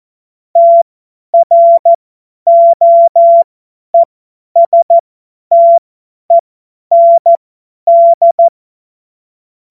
Morse code, heard as a keyed tone, about 14 wpm, spaced as for 10 wpm, 680 Hz. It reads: TROESTEND
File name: Morsekurs / Witze / Witz 08. Februar 2022 50 F70